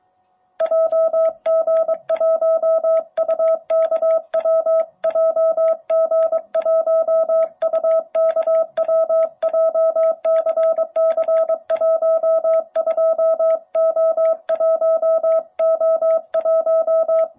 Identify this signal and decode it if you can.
Morse code, heard as a keyed tone, about 23 wpm, 660 Hz. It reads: JG1UKW JG1UKW JCC12O1O1
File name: jh8cbh / 2021cw